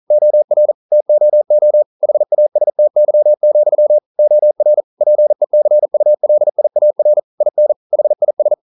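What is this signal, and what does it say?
Morse code, heard as a keyed tone, about 42 wpm, 600 Hz. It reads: OR TOO HASTY, OR PECULIAR IN HIS
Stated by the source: Morse code practice at 41 wpm, from CW med Gnister.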